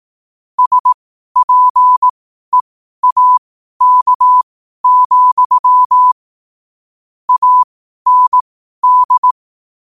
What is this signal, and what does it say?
Morse code, heard as a keyed tone, about 18 wpm, spaced as for 12 wpm, 1000 Hz. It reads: SPEAK, AND
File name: qotdfiles / 12wpm